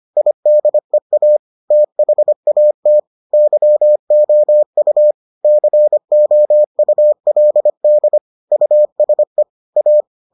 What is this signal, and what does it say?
Morse code, heard as a keyed tone, about 25 wpm, 600 Hz. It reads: IDEA THAT YOU COULD USE A